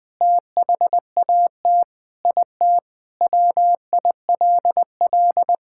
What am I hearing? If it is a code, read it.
Morse code, heard as a keyed tone, about 20 wpm, 700 Hz. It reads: THAT IT WILL
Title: Sentences with 3-Word Phrases - 20wpm